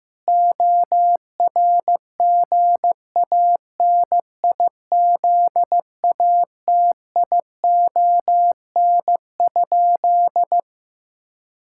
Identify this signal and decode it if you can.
Morse code, heard as a keyed tone, about 16 wpm, 700 Hz. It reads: ORGANIZATION?
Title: Sentences from Top 1000 Words - Review Part 5 - 15wpm